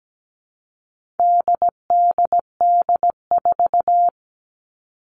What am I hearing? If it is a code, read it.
Morse code, heard as a keyed tone, about 17 wpm, 700 Hz. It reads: DDD4
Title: All Cut Numbers - 4-Digit 17wpm